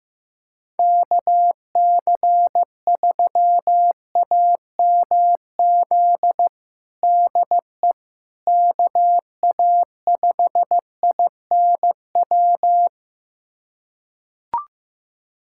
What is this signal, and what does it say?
Morse code, heard as a keyed tone, about 15 wpm, 700 Hz. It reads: KC3AMZ DE KA5INW
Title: USA Callsign DE Callsign - 6 Characters - 15wpm